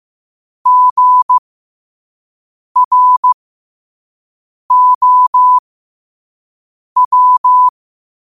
Morse code, heard as a keyed tone, about 15 wpm, spaced as for 5 wpm, 1000 Hz. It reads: GROW
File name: qotdfiles / 5wpm